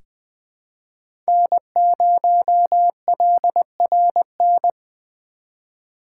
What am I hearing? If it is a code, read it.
Morse code, heard as a keyed tone, about 20 wpm, 700 Hz. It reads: N0LRN
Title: USA Call Signs - 5 Characters - 4x Character Spacing - 20wpm